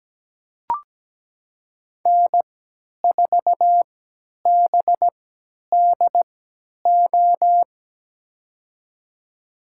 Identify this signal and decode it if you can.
Morse code, heard as a keyed tone, about 17 wpm, spaced as for 10 wpm, 700 Hz. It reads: N4BDO
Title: USA Call Signs - 3x Character Spacing - 17wpm